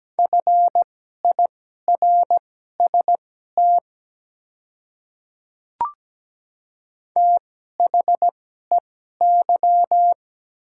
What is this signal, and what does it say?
Morse code, heard as a keyed tone, about 17 wpm, spaced as for 12 wpm, 690 Hz. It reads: FIRST THEY
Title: 2x Character Spacing - Top 100 Words - 17wpm